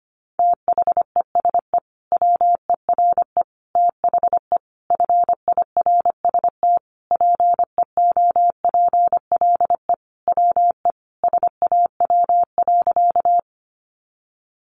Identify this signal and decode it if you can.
Morse code, heard as a keyed tone, about 25 wpm, 700 Hz. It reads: THESE WERE THE FIRST PEOPLE WE SAW.
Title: Sentences from Top 300 Words 25wpm